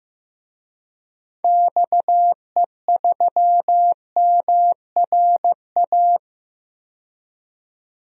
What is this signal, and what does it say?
Morse code, heard as a keyed tone, about 15 wpm, 700 Hz. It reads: XE3MRA
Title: DXCC-100 Call Signs - 6 Characters - 15wpm